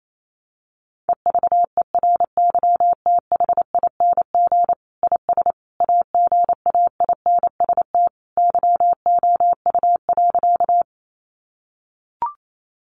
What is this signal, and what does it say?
Morse code, heard as a keyed tone, about 28 wpm, 700 Hz. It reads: EVERYTHING IS AGAINST YOU.
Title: Sentences from Top 700 Words 28wpm